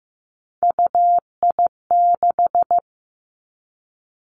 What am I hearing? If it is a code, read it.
Morse code, heard as a keyed tone, about 15 wpm, 700 Hz. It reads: UI6
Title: Three Challenging Letter-Number Combinations 15wpm